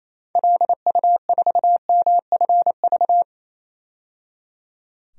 Morse code, heard as a keyed tone, about 28 wpm, 700 Hz. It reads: LU4MFV
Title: DXCC-100 Call Signs - 6 Characters - 28wpm